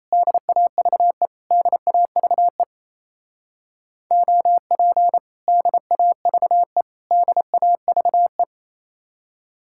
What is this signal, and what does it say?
Morse code, heard as a keyed tone, about 34 wpm, 700 Hz. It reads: DAVE DAVE OP DAVE DAVE
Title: Speed-Racing Common QSO Phrases - 28wpm